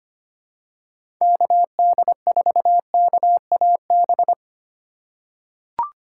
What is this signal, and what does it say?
Morse code, heard as a keyed tone, about 25 wpm, 700 Hz. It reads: KD4KAB
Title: USA Call Signs - 6 Characters - 4x Character Spacing - 25wpm